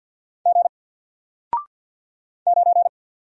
Morse code, heard as a keyed tone, about 48 wpm, 700 Hz. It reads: G 9
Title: Single Character (Letter:Number:Punctuation:Prosign) Alt 50wpm